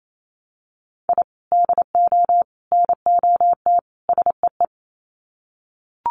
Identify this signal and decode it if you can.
Morse code, heard as a keyed tone, about 28 wpm, 700 Hz. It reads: I DO NOT SEE
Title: Sets of 4 Words 28wpm